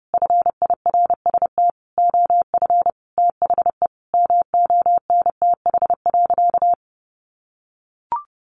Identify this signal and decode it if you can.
Morse code, heard as a keyed tone, about 28 wpm, 700 Hz. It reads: FIRST OF THE MONTH.
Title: Sentences from Top 800 Words 30wpm